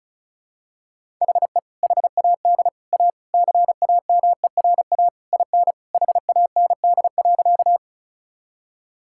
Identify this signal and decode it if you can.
Morse code, heard as a keyed tone, about 34 wpm, 700 Hz. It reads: HE HAD A CAMERA IN HAND.
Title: Sentences from Top 1000 Words - Review Part 5 - 35wpm